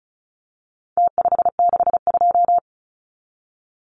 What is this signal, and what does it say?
Morse code, heard as a keyed tone, about 34 wpm, 700 Hz. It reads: T562
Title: Numbers - Contest Style - 4-Digit 35wpm